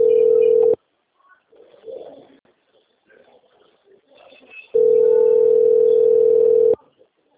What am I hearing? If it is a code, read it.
Ringback tone (North America), 440 + 480 Hz.